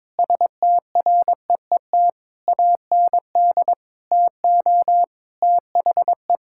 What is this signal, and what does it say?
Morse code, heard as a keyed tone, about 22 wpm, 700 Hz. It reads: STREET AND TO THE
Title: Sentences from Top 1600 Words 22wpm